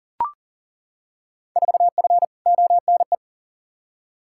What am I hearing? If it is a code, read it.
Morse code, heard as a keyed tone, about 40 wpm, 700 Hz. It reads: 4F ONE
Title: Field Day Exchanges Encore - 40wpm